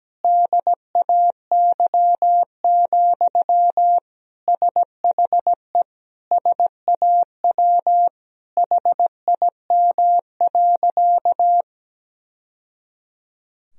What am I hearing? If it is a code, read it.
Morse code, heard as a keyed tone, about 17 wpm, 700 Hz. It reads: DAY, SHE SAW HIM.
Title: Sentences from Top 1000 Words - Review Part 4 - 17wpm